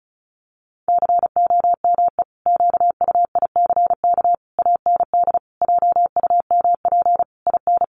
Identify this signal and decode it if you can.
Morse code, heard as a keyed tone, about 34 wpm, 700 Hz. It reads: COME QUICK AND JUMP IN
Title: Sentences from Top 1300 Words 35wpm